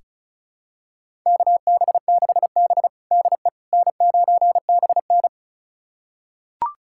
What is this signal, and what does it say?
Morse code, heard as a keyed tone, about 36 wpm, 700 Hz. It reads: KB6B DE N9BN
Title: USA Callsign DE Callsign - 4 Characters - 35wpm